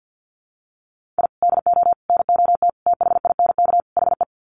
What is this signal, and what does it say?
Morse code, heard as a keyed tone, about 50 wpm, 700 Hz. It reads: I DO NOT THINK HE
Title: Sets of 5 Words 50wpm